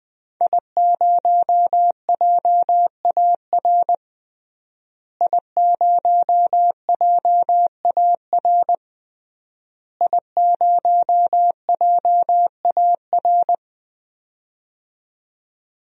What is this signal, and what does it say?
Morse code, heard as a keyed tone, about 20 wpm, 700 Hz. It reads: I0JAR I0JAR I0JAR
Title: DXCC-100 Call Signs - 5 Characters - Repeated 3x - 20wpm